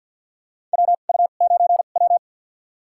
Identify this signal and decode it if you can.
Morse code, heard as a keyed tone, about 52 wpm, spaced as for 36 wpm, 700 Hz. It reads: WU9W
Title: USA Call Signs - 4 Characters - 2x Character Spacing - 50wpm